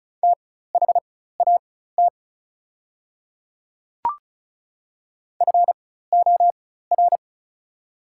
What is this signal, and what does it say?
Morse code, heard as a keyed tone, about 34 wpm, spaced as for 16 wpm, 700 Hz. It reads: THAT FOR
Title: 4x Character Spacing - Top 100 Words - 35wpm